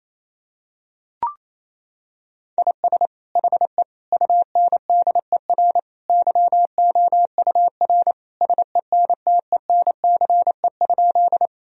Morse code, heard as a keyed tone, about 28 wpm, 700 Hz. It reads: IS HE UNDER YOUR SENTENCE?
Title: Sentences from Top 1000 Words - Review Part 1 - 28wpm